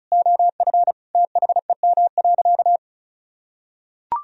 Morse code, heard as a keyed tone, about 36 wpm, 700 Hz. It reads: OF THEM.
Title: Sentences from Top 1000 Words - Review Part 2 - 35wpm